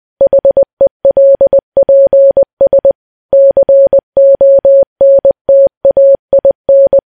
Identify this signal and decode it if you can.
Morse code, heard as a keyed tone, about 20 wpm, 550 Hz. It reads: HELPS CONTAIN
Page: news-headlines-in-morse-code-at-20-wpm